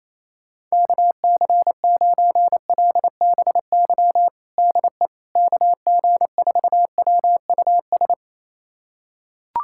Morse code, heard as a keyed tone, about 28 wpm, 700 Hz. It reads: KC9LBY DE KG4WUS
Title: USA Callsign DE Callsign - 6 Characters - 28wpm